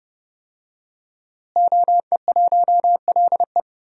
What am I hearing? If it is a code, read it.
Morse code, heard as a keyed tone, about 30 wpm, 700 Hz. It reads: OE1LE
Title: DXCC-100 Call Signs - 5 Characters - Repeated 3x - 30wpm